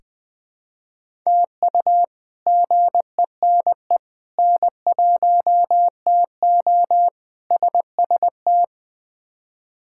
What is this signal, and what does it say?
Morse code, heard as a keyed tone, about 20 wpm, 700 Hz. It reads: TU GENE N1TO SST
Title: K1USN SST QSO Practice - 20wpm